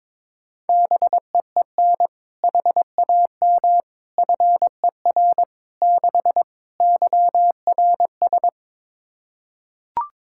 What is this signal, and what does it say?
Morse code, heard as a keyed tone, about 22 wpm, 700 Hz. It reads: BEEN HAM FER 6 YRS
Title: K1USN SST QSO Practice 22wpm